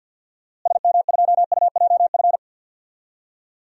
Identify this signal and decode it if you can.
Morse code, heard as a keyed tone, about 47 wpm, 700 Hz. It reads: SM2UJF
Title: DXCC-100 Call Signs - 6 Characters - Repeated 2x - 50wpm